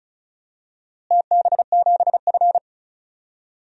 Morse code, heard as a keyed tone, about 34 wpm, 700 Hz. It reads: TB7F
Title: DXCC-100 Call Signs - 4 Characters - Repeated 2x - 35wpm